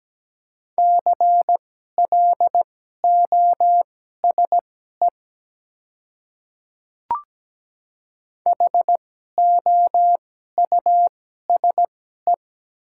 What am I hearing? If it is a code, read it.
Morse code, heard as a keyed tone, about 17 wpm, spaced as for 12 wpm, 690 Hz. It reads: CLOSE HOUSE
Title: 2x Character Spacing - Top 300 Words - 17wpm